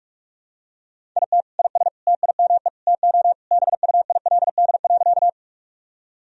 Morse code, heard as a keyed tone, about 46 wpm, 700 Hz. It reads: IT IS TIME TO BUILD.
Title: Sentences from Top 1000 Words - Review Part 1 - 45wpm